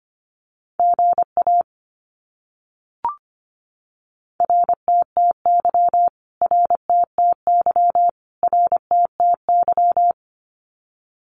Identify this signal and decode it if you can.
Morse code, heard as a keyed tone, about 25 wpm, 700 Hz. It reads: GA RTTY RTTY RTTY
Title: Abbreviations Repeated 3x - 25wpm